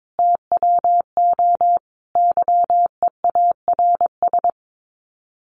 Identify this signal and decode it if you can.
Morse code, heard as a keyed tone, about 22 wpm, 700 Hz. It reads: TWO YEARS
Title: Sets of 5 Words Encore2 22wpm